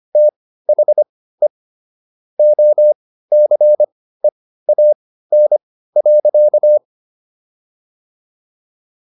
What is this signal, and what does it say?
Morse code, heard as a keyed tone, about 25 wpm, spaced as for 15 wpm, 600 Hz. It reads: THE OCEAN.